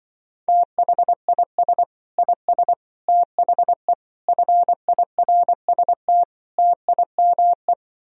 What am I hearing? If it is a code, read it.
Morse code, heard as a keyed tone, about 24 wpm, 700 Hz. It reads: THIS IS THE FIRST TIME
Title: Speed-Racing - Sets of 5 Words 20wpm